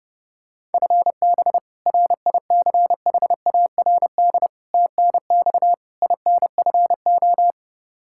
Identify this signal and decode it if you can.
Morse code, heard as a keyed tone, about 30 wpm, 700 Hz. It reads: FB RICHARD TNX INFO
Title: Practice QSO - Exchange 3 - More Info - 30wpm